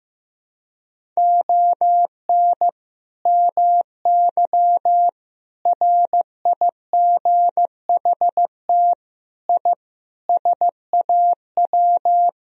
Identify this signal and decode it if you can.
Morse code, heard as a keyed tone, about 15 wpm, 700 Hz. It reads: ON MY RIGHT I SAW
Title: Sentences from Top 1000 Words - Review Part 2 - 15wpm